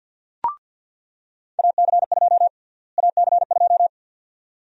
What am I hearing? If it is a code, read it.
Morse code, heard as a keyed tone, about 46 wpm, 700 Hz. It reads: AC2 AC2
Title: USA Call Sign Prefixes - Repeated 2x - 50wpm